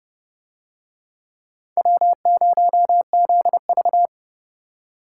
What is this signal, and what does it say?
Morse code, heard as a keyed tone, about 30 wpm, 700 Hz. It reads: W0ZV